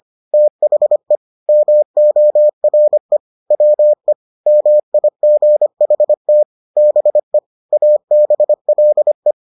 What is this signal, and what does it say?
Morse code, heard as a keyed tone, about 25 wpm, 600 Hz. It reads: THE MORE WE MIGHT BE ABLE